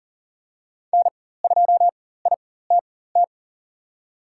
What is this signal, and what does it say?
Morse code, heard as a keyed tone, about 40 wpm, spaced as for 19 wpm, 700 Hz. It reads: N2ITT
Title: USA Call Signs - 5 Characters - 4x Character Spacing - 40wpm